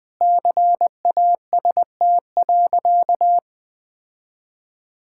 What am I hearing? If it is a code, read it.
Morse code, heard as a keyed tone, about 20 wpm, 700 Hz. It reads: CAST.
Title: All-Age Appropriate Jokes - 20wpm